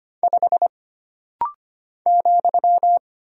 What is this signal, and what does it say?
Morse code, heard as a keyed tone, about 25 wpm, 700 Hz. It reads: 5 ,
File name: Single Character (Letter:Number:Punctuation:Prosign) - Rapid-Fire - 25wpm